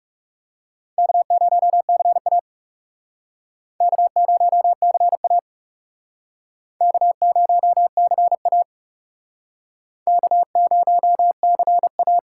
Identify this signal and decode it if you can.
Morse code, heard as a keyed tone, about 36 wpm, 700 Hz. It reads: K0CA K0CA K0CA K0CA
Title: Speed Racing - USA 1x2 Call Signs 30wpm